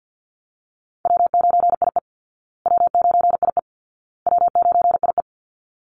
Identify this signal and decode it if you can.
Morse code, heard as a keyed tone, about 50 wpm, 700 Hz. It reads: W9IE W9IE W9IE
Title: Call Signs Repeated 2x - 50wpm